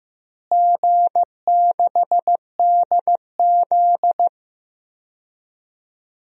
Morse code, heard as a keyed tone, about 15 wpm, 700 Hz. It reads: G6DZ
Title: DXCC-100 Call Signs - 4 Characters - Repeated 3x - 15wpm